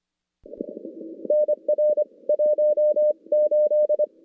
Morse code, heard as a keyed tone, about 26 wpm, 600 Hz. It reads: NR 18
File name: contests / CW-SS-k5tr- -2100